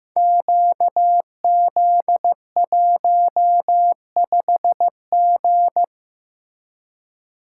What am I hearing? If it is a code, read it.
Morse code, heard as a keyed tone, about 15 wpm, 700 Hz. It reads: QZ15G
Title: Code Groups - 5 Letter:Numbers - 15wpm